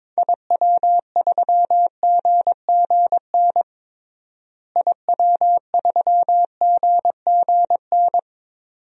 Morse code, heard as a keyed tone, about 22 wpm, 700 Hz. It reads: IW3GGN IW3GGN
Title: DXCC-100 Call Signs - 6 Characters - Repeated 2x - 22wpm